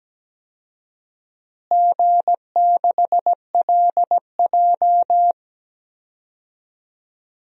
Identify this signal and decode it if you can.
Morse code, heard as a keyed tone, about 17 wpm, 700 Hz. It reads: G6LJ
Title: DXCC-100 Call Signs - 4 Characters - Repeated 3x - 17wpm